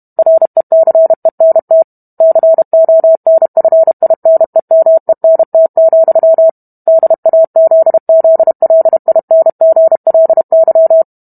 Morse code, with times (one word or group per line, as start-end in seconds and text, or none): 0.2-1.8 RECENT
2.2-6.5 CONFINEMENT,
6.9-11.0 DAZZLINGLY